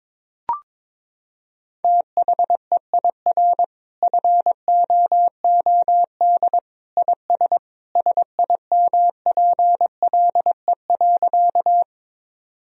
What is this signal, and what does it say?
Morse code, heard as a keyed tone, about 21 wpm, 700 Hz. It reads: THEIR FOOD IS SIMPLE.
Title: Sentences from Top 1000 Words - Review Part 1 - 22wpm